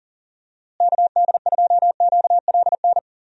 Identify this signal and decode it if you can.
Morse code, heard as a keyed tone, about 40 wpm, 700 Hz. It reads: KD2QLN
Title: USA Call Signs - 6 Characters - Repeated 2x - 40wpm